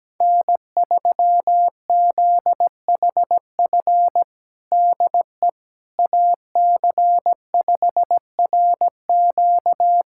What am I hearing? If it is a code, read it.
Morse code, heard as a keyed tone, about 17 wpm, 700 Hz. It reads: N3ZHF DE AC5RQ
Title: USA Callsign DE Callsign - 5 Characters - 17wpm